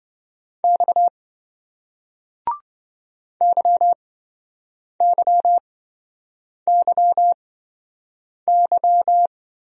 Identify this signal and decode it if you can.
Morse code, heard as a keyed tone, about 28 wpm, spaced as for 7 wpm, 700 Hz. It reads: X YYYY